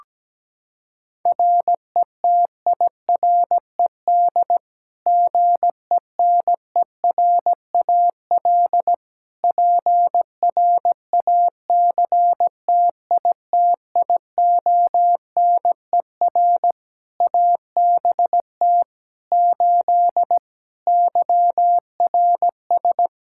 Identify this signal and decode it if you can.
Morse code, heard as a keyed tone, about 17 wpm, 700 Hz. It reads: RETIRED GENERAL PRACTITIONER ABT 8 YRS
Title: Practice QSO - Exchange 3 - More Info - 17wpm